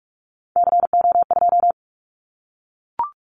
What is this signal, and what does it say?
Morse code, heard as a keyed tone, about 44 wpm, 700 Hz. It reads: CO2
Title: Callsign Prefixes for Top 100 DXCC - 45wpm